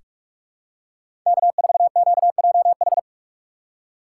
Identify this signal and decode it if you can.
Morse code, heard as a keyed tone, about 44 wpm, 700 Hz. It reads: K4QJH
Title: USA Call Signs - 5 Characters - 4x Character Spacing - 45wpm